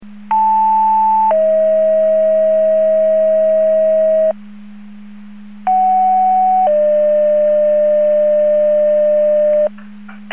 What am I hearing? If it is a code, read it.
Two-tone page, heard as a one-second tone, then a three-second tone, about 880 then 640 Hz, about 750 then 600 Hz. It is heard from 0.3 to 9.7 s.